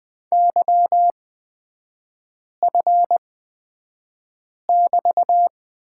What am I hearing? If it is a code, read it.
Morse code, heard as a keyed tone, about 20 wpm, spaced as for 5 wpm, 700 Hz. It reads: YF<BT>
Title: 224 Lesson 36e - BT - Three Characters 20wpm